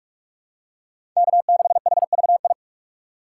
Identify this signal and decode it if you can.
Morse code, heard as a keyed tone, about 46 wpm, 700 Hz. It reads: K6HVI
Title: USA Call Signs - 5 Characters - 3x Character Spacing - 45wpm